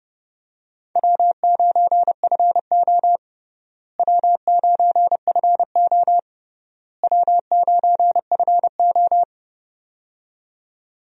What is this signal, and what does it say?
Morse code, heard as a keyed tone, about 30 wpm, 700 Hz. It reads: W9FO W9FO W9FO
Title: USA Call Signs - 4 Characters - Repeated 3x - 30wpm